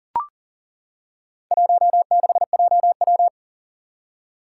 Morse code, heard as a keyed tone, about 40 wpm, 700 Hz. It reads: 16JW
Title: Four Challenging Letter-Number Combinations 40wpm